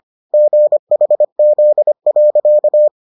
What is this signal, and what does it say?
Morse code, heard as a keyed tone, about 25 wpm, 600 Hz. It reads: GHZ.